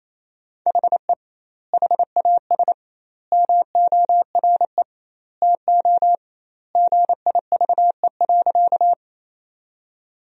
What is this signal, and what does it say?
Morse code, heard as a keyed tone, about 28 wpm, 700 Hz. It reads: HE HAS MORE TO GIVE.
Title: 2x Word-Spacing - Sentences from Top 200 Words - 28wpm